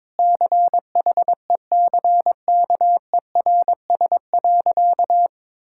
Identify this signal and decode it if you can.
Morse code, heard as a keyed tone, about 22 wpm, 690 Hz. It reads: CHECKERS.
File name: practice / 22wpm / Easy Sentences with Hints - 22wpm